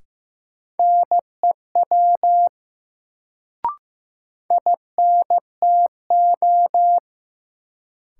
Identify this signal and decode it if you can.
Morse code, heard as a keyed tone, about 15 wpm, 700 Hz. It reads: NEW INTO
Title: Sets of 2 Words - New Word Drill - 15wpm